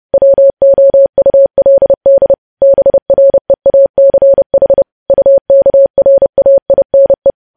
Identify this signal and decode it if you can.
Morse code, heard as a keyed tone, about 30 wpm, 550 Hz. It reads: WOULD BREACH UKRAINE
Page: news headlines in morse code at 30 wpm